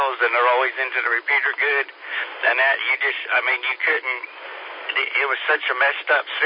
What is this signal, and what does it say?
Emergency Alert System (SAME) tones (header bursts), heard as harsh buzzing bursts, with speech. It is heard from 0.9 to 5.6 s.